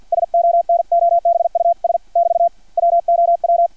Morse code, heard as a keyed tone, about 56 wpm, spaced as for 49 wpm, 690 Hz. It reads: SONOBUS <BT> WOW
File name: archived / CRSnet